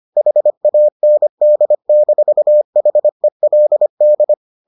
Morse code, heard as a keyed tone, about 25 wpm, 600 Hz. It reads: HAND-HELD